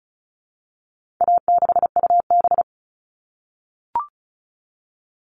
Morse code, heard as a keyed tone, about 34 wpm, 700 Hz. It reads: A6UB
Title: All Cut Numbers - 4-Digit 35wpm